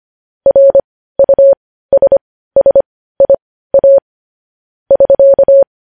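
Morse code, heard as a keyed tone, about 25 wpm, spaced as for 15 wpm, 550 Hz. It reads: RUSSIA <SK>
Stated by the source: Morse code news headlines at 15.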